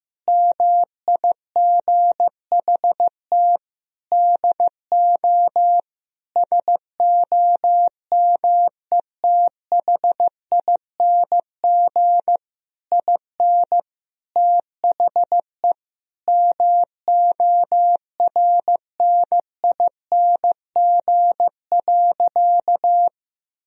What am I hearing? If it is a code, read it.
Morse code, heard as a keyed tone, about 15 wpm, 700 Hz. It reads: MIGHT DO SOMETHING IN THE MORNING.
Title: Sentences from Top 1000 Words - Review Part 4 - 15wpm